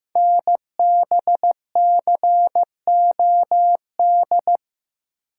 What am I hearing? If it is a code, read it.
Morse code, heard as a keyed tone, about 15 wpm, 700 Hz. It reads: NBCOD
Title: Code Groups - 5 Letters - 15wpm